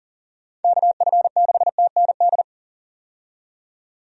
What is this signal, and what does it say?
Morse code, heard as a keyed tone, about 40 wpm, 700 Hz. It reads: KF6TND
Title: USA Call Signs - 6 Characters - 40wpm